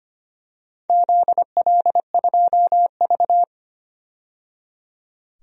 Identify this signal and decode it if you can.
Morse code, heard as a keyed tone, about 25 wpm, 700 Hz. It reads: ZL2V
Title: DXCC-100 Call Signs - 4 Characters - 25wpm